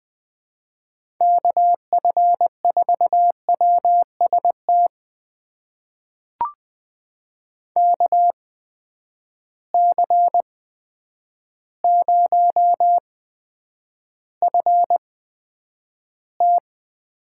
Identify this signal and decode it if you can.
Morse code, heard as a keyed tone, about 20 wpm, spaced as for 9 wpm, 700 Hz. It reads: KF4WST KC0FT
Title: USA Call Signs - 6 Characters - 8x Character Spacing - 20wpm